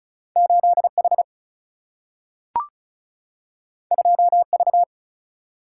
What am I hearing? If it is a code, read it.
Morse code, heard as a keyed tone, about 34 wpm, 700 Hz. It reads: 8H 2V
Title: Single-Letter Single-Number 35wpm